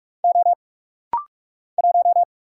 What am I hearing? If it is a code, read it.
Morse code, heard as a keyed tone, about 44 wpm, 700 Hz. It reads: O 1
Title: Single Letter-Number - Rapid-Fire 45wpm